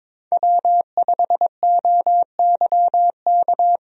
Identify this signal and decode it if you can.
Morse code, heard as a keyed tone, about 22 wpm, 700 Hz. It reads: W5OYK
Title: USA Call Signs - 5 Characters - 22wpm